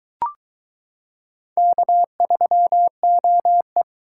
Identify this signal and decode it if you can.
Morse code, heard as a keyed tone, about 22 wpm, 700 Hz. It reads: K3OE